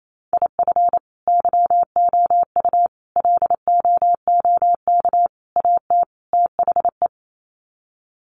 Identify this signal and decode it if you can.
Morse code, heard as a keyed tone, about 28 wpm, 700 Hz. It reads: IF YOU LOOK AT THE
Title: Sets of 5 Words 28wpm